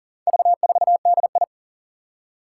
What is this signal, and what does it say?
Morse code, heard as a keyed tone, about 42 wpm, 700 Hz. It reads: V4DI
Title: Four Challenging Letter-Number Combinations 40wpm